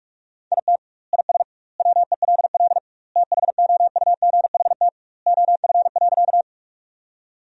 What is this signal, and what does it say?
Morse code, heard as a keyed tone, about 46 wpm, 700 Hz. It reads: IT IS WELL THOUGHT OF.
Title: 2x Word-Spacing - Sentences from Top 300 Words - 45wpm